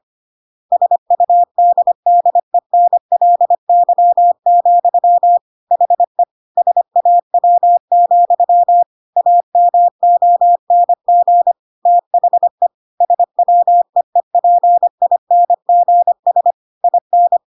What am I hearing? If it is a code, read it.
Morse code, heard as a keyed tone, about 25 wpm, 700 Hz. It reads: SUDDENLY, HE SAW, AMONG THE SWEEPINGS IN
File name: Pinocchio / Pinocchio ch5 25 25 ik1hge